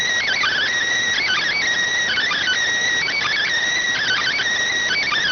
Emergency Alert System (SAME) tones, heard as harsh buzzing bursts. It is heard from 0.2 to 4.4 s.